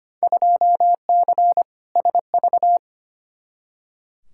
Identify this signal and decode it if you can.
Morse code, heard as a keyed tone, about 25 wpm, 700 Hz. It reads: 2C SV
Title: Field Day Exchanges Encore - 25wpm